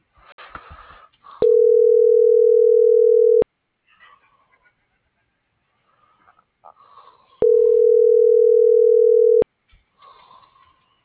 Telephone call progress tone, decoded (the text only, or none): ringback tone